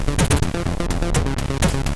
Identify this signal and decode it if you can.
There is no code.